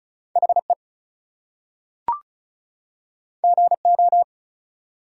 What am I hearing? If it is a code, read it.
Morse code, heard as a keyed tone, about 36 wpm, 700 Hz. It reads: HE GO
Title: Two Letter Words - 35wpm